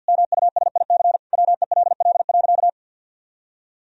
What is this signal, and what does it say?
Morse code, heard as a keyed tone, about 50 wpm, 700 Hz. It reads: MUSIC WELL.
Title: Sentences from Top 1000 Words - Review Part 1 - 50wpm